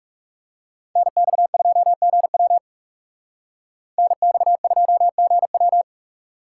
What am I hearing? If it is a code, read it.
Morse code, heard as a keyed tone, about 39 wpm, 700 Hz. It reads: NX2GW NX2GW